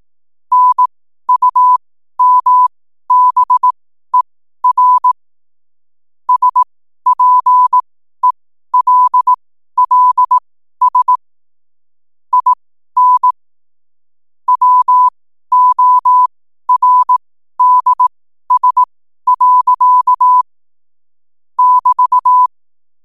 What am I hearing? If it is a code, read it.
Morse code, heard as a keyed tone, about 18 wpm, spaced as for 12 wpm, 1000 Hz. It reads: NUMBER SPELLS IN WORDS. <BT>
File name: n7net / 12wpm